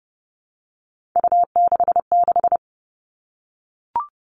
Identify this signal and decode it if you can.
Morse code, heard as a keyed tone, about 30 wpm, 700 Hz. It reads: U66
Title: Three Challenging Letter-Number Combinations 30wpm